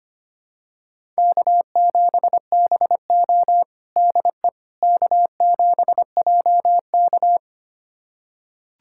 Morse code, heard as a keyed tone, about 25 wpm, 700 Hz. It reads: K7BO DE K7JK
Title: USA Callsign DE Callsign - 4 Characters - 25wpm